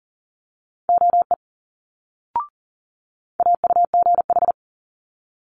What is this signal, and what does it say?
Morse code, heard as a keyed tone, about 38 wpm, 700 Hz. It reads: OE AUGH
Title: Multi-letter Phonemes Quick Fire 40wpm